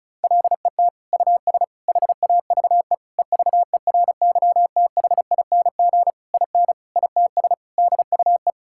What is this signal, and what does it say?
Morse code, heard as a keyed tone, about 36 wpm, 700 Hz. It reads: LET US HAVE EVERYTHING IN ITS DUE
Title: Sentences from Top 1000 Words - Review Part 6 - 35wpm